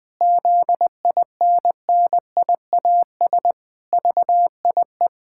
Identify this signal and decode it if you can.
Morse code, heard as a keyed tone, about 20 wpm, 700 Hz. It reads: ZINNIAS VIE
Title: Pangram Challenge v2 - 20wpm